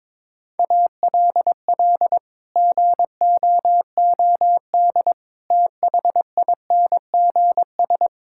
Morse code, heard as a keyed tone, about 21 wpm, 700 Hz. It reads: ALL GOOD THINGS
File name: practice / 22wpm / Sets of 8 Words - 22wpm